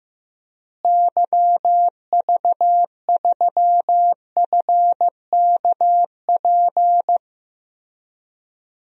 Morse code, heard as a keyed tone, about 15 wpm, 700 Hz. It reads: YV3FKP
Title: DXCC-100 Call Signs - 6 Characters - Repeated 2x - 15wpm